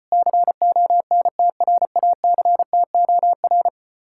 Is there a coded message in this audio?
Yes: Morse code from 0.1 to 3.7 s, heard as a keyed tone, about 33 wpm, 700 Hz. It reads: CONTRACTOR